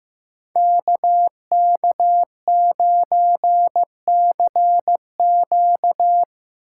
Morse code, heard as a keyed tone, about 17 wpm, 700 Hz. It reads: KK9CQ
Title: Speed Racing - USA 2x2 Call Signs 15wpm